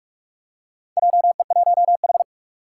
Morse code, heard as a keyed tone, about 46 wpm, 700 Hz. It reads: JE1H
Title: Four Challenging Letter-Number Combinations 45wpm